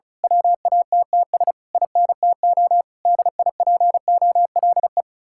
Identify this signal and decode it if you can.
Morse code, heard as a keyed tone, about 33 wpm, 700 Hz. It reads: WATTS INTO DIPOLE
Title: Speed-Racing Common QSO Phrases Encore - 35wpm